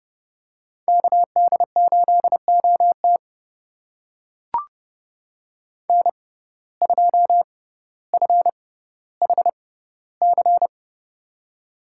Morse code, heard as a keyed tone, about 30 wpm, spaced as for 15 wpm, 700 Hz. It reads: KD8OT N2FHC
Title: USA Call Signs - 5 Characters - 6x Character Spacing - 30wpm